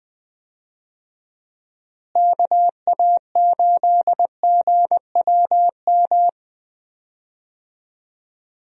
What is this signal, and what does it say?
Morse code, heard as a keyed tone, about 20 wpm, 700 Hz. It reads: KA8GWM
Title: USA Call Signs - 6 Characters - Repeated 2x - 20wpm